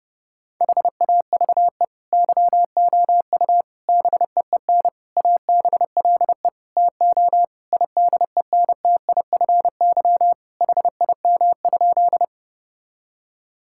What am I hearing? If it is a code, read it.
Morse code, heard as a keyed tone, about 30 wpm, 700 Hz. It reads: HAVE YOU BEEN ABLE TO IDENTIFY HIM?
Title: Sentences from Top 1000 Words - Review Part 6 - 30wpm